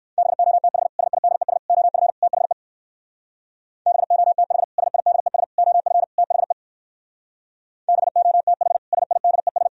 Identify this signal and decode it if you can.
Morse code, heard as a keyed tone, about 61 wpm, 700 Hz. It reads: BOTH SIDES OF THE BOTH SIDES OF THE BOTH SIDES